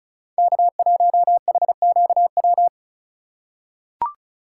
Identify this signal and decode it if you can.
Morse code, heard as a keyed tone, about 36 wpm, 700 Hz. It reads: K1HQW
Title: USA Call Signs - 4x Character Spacing - 35wpm